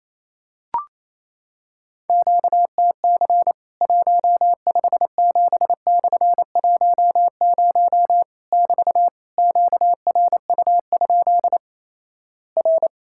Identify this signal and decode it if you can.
Morse code, heard as a keyed tone, about 28 wpm, 700 Hz. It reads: QTC 157/10 <BT> QRU? R
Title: Worked All Europe DX Contest - QTC Exchange - 28wpm